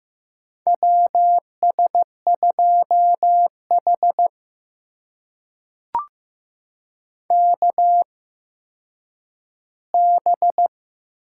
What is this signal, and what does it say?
Morse code, heard as a keyed tone, about 15 wpm, spaced as for 9 wpm, 700 Hz. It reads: WS2H KB
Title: USA Call Signs - 8x Character Spacing - 15wpm